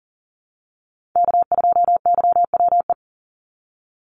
Morse code, heard as a keyed tone, about 41 wpm, 700 Hz. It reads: K2YWE
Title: Call Signs - 40wpm